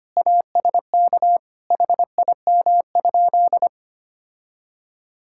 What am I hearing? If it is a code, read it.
Morse code, heard as a keyed tone, about 25 wpm, 700 Hz. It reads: ASK HIM?
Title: Sentences from Top 1600 Words 25wpm